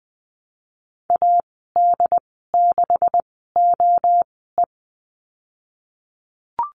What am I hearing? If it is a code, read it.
Morse code, heard as a keyed tone, about 20 wpm, spaced as for 15 wpm, 700 Hz. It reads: AD6OE
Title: 161 Lesson 26g - Letter g - Call Signs 20wpm